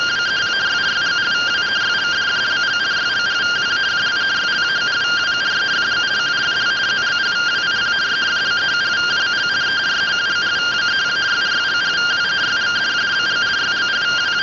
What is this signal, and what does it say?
Two-tone teleprinter signal, about 170 Hz shift.
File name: findings / axial iodine 3mm 205.gen-10 rtty rx